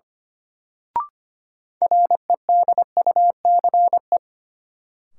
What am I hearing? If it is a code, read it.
Morse code, heard as a keyed tone, about 25 wpm, 700 Hz. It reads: REDUCE
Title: Sentences from Top 1100 Words - New Word Drill Alt - 25wpm